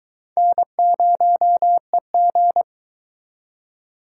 Morse code, heard as a keyed tone, about 24 wpm, 700 Hz. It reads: N0EG